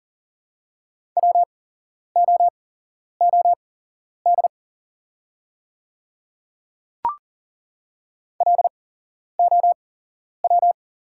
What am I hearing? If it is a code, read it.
Morse code, heard as a keyed tone, about 40 wpm, spaced as for 11 wpm, 700 Hz. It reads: WOOD LOW
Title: 8x Character Spacing - Top 500 Words - 40wpm